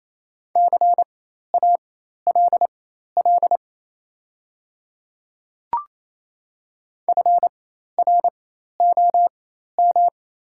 Morse code, heard as a keyed tone, about 27 wpm, spaced as for 13 wpm, 690 Hz. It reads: CALL FROM
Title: 4x Character Spacing - Top 100 Words - 28wpm